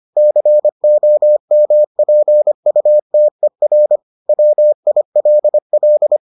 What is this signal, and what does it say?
Morse code, heard as a keyed tone, about 25 wpm, 600 Hz. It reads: COMPUTER WILL